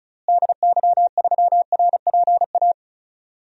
Morse code, heard as a keyed tone, about 36 wpm, 700 Hz. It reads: DY3RPA
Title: DXCC-100 Call Signs - 6 Characters - Repeated 2x - 35wpm